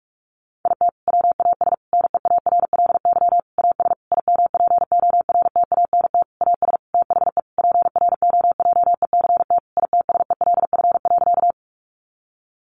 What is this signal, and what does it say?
Morse code, heard as a keyed tone, about 44 wpm, 700 Hz. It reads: IT WAS NEARLY AS IMPORTANT AS THE PROJECT ITSELF.
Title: Sentences from Top 1400 Words 45wpm